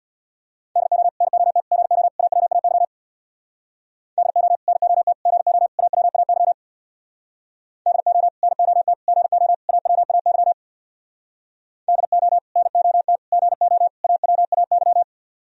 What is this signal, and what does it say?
Morse code, heard as a keyed tone, about 60 wpm, 700 Hz. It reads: DO NOT GO AWAY DO NOT GO AWAY DO NOT GO AWAY DO NOT GO AWAY
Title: Speed-Racing - Sets of 4 Words 50wpm